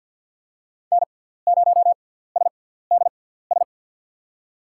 Morse code, heard as a keyed tone, about 50 wpm, spaced as for 17 wpm, 700 Hz. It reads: N0SDS
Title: USA Call Signs - 6x Character Spacing - 50wpm